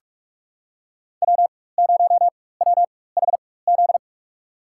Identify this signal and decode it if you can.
Morse code, heard as a keyed tone, about 47 wpm, spaced as for 21 wpm, 700 Hz. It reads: W0WHZ
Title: USA Call Signs - 5 Characters - 4x Character Spacing - 45wpm